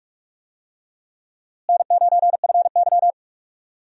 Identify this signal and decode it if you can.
Morse code, heard as a keyed tone, about 43 wpm, 700 Hz. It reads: N9FY
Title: USA Call Signs - 4 Characters - Repeated 2x - 45wpm